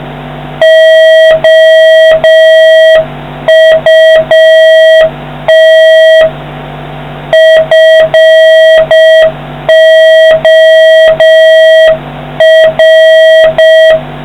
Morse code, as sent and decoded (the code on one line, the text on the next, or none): --- ..- - / ..-. --- .-.
OUT FOR